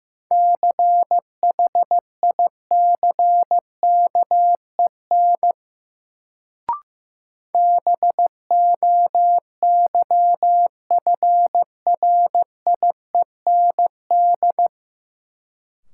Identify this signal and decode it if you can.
Morse code, heard as a keyed tone, about 15 wpm, 700 Hz. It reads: CHICKEN BOYFRIEND
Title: Sentences from Top 1000 Words - Review Part 5 - New Word Drill - 15wpm